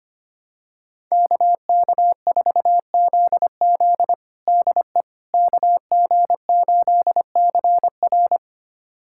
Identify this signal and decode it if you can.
Morse code, heard as a keyed tone, about 25 wpm, 700 Hz. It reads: KK4ZZ DE KG8CR
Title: USA Callsign DE Callsign - 5 Characters - 25wpm